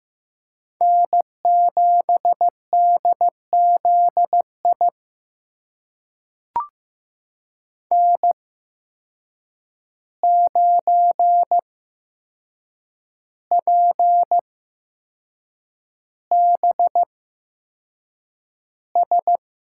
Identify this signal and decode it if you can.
Morse code, heard as a keyed tone, about 15 wpm, spaced as for 6 wpm, 700 Hz. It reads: N7DZI N9PBS
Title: USA Call Signs - 5 Characters - 8x Character Spacing - 15wpm